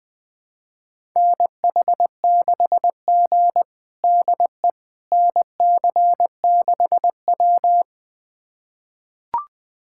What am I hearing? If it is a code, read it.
Morse code, heard as a keyed tone, about 20 wpm, 700 Hz. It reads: NH6G DE NC6W
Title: USA Callsign DE Callsign - 4 Characters - 20wpm